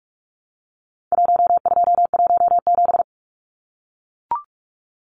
Morse code, heard as a keyed tone, about 44 wpm, 700 Hz. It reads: 1217
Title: Numbers - Contest Style - 4-Digit 45wpm